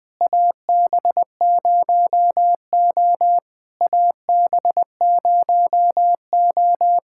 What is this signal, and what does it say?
Morse code, heard as a keyed tone, about 20 wpm, 700 Hz. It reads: AB0O AB0O
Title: POTA QSO Practice - 20wpm